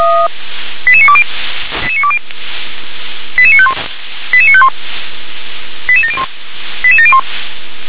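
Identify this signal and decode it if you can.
Five-tone selective call (ZVEI), heard as quick runs of five tones, tones of about 70 ms. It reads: 88920 88951 88951 88701 88701
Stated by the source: ZVEI five-tone selective call received by SDR.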